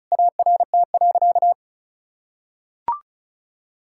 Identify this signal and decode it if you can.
Morse code, heard as a keyed tone, about 36 wpm, 700 Hz. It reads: ART.
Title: Sentences from Top 1000 Words - Review Part 6 - 35wpm